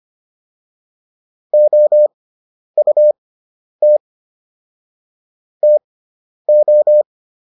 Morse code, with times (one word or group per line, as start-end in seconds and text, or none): 1.5-4.0 OUT
5.6-7.0 TO